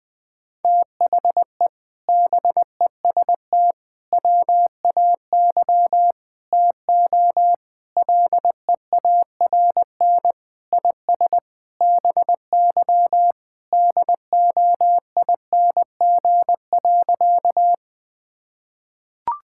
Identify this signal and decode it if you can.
Morse code, heard as a keyed tone, about 20 wpm, 700 Hz. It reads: THE BEST WAY TO LEARN IS BY DOING.
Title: Sentences with 4-Word Phrases - 20wpm